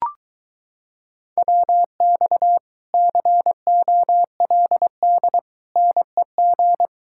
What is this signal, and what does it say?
Morse code, heard as a keyed tone, about 23 wpm, 700 Hz. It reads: WX COLD NEG